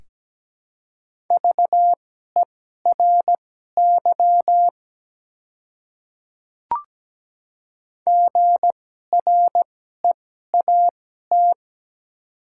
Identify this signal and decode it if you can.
Morse code, heard as a keyed tone, about 17 wpm, spaced as for 12 wpm, 700 Hz. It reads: VERY GREAT
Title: 2x Character Spacing - Top 200 Words - 17wpm